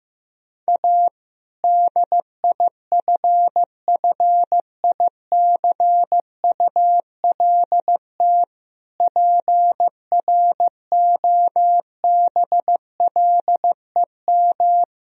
Morse code, heard as a keyed tone, about 15 wpm, 700 Hz. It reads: A DIFFICULT PROBLEM
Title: Sentences from Top 1000 Words - Review Part 2 - 15wpm